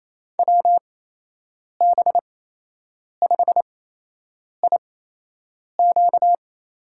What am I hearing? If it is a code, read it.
Morse code, heard as a keyed tone, about 28 wpm, spaced as for 8 wpm, 700 Hz. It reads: WB5IQ
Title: USA Call Signs - 8x Character Spacing - 28wpm